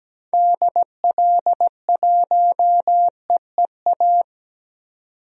Morse code, heard as a keyed tone, about 17 wpm, 700 Hz. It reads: DL1EEA
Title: DXCC-100 Call Signs - 6 Characters - 17wpm